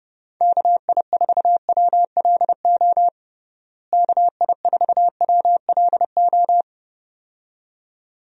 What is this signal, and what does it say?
Morse code, heard as a keyed tone, about 30 wpm, 700 Hz. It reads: KI4WLO KI4WLO
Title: USA Call Signs - 6 Characters - Repeated 2x - 30wpm